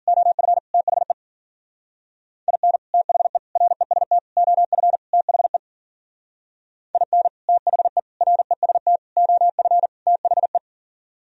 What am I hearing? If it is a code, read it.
Morse code, heard as a keyed tone, about 43 wpm, 700 Hz. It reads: OF THE IN THE REST OF THE IN THE REST OF THE
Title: Speed-Racing - Sets of 5 Words 40wpm